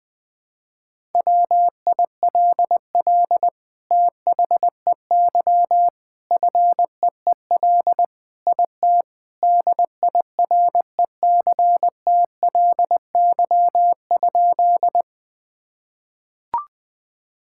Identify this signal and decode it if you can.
Morse code, heard as a keyed tone, about 20 wpm, 700 Hz. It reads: WILL THEY FEEL IT DIRECTLY?
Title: Sentences from Top 1000 Words - Review Part 5 - 20wpm